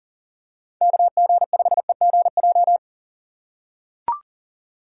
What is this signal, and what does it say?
Morse code, heard as a keyed tone, about 40 wpm, 700 Hz. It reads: KG5EGJ
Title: USA Call Signs - 6 Characters - 2x Character Spacing - 40wpm